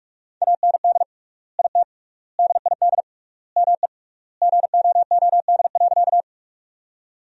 Morse code, heard as a keyed tone, about 46 wpm, 700 Hz. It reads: AND IT DID ME GOOD.
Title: 3x Word-Spacing - Sentences from Top 200 Words - 45wpm